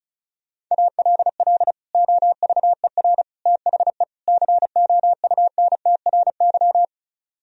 Morse code, heard as a keyed tone, about 44 wpm, 700 Hz. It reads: ALL OVER THE COUNTRY